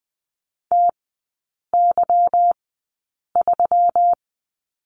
Morse code, heard as a keyed tone, about 20 wpm, spaced as for 8 wpm, 700 Hz. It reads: TY3
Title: 152 Lesson 25e - Letter p - Three Characters 20wpm